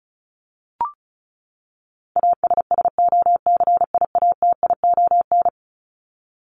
Morse code, heard as a keyed tone, about 34 wpm, 700 Hz. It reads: ASSOCIATION
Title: Top 1001-1250 Words 35wpm